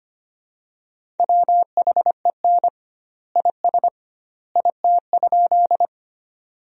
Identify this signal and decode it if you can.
Morse code, heard as a keyed tone, about 25 wpm, 700 Hz. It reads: WHEN IS IT?
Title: 2x Word-Spacing - Sentences from Top 100 Words - 25wpm